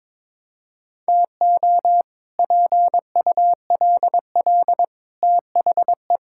Morse code, heard as a keyed tone, about 22 wpm, 700 Hz. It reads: TO PULL THE
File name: practice / 22wpm / Sets of 8 Words - 22wpm